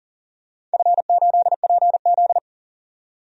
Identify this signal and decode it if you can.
Morse code, heard as a keyed tone, about 40 wpm, 700 Hz. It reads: F8PZ
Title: DXCC-100 Call Signs - 4 Characters - 40wpm